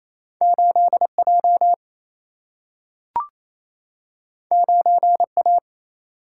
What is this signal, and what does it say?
Morse code, heard as a keyed tone, about 28 wpm, 700 Hz. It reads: 8 J 9 A
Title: Single-Letter Single-Number 28wpm